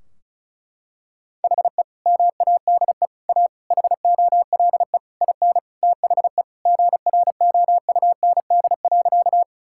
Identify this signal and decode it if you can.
Morse code, heard as a keyed tone, about 36 wpm, 700 Hz. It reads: HE MADE A HOLE IN THE GROUND.
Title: Sentences from Top 1600 Words 35wpm